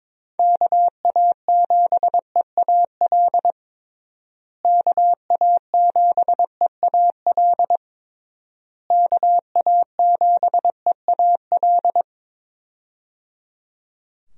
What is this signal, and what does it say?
Morse code, heard as a keyed tone, about 22 wpm, 700 Hz. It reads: KA7EAL KA7EAL KA7EAL
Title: USA Call Signs - 6 Characters - Repeated 3x - 22wpm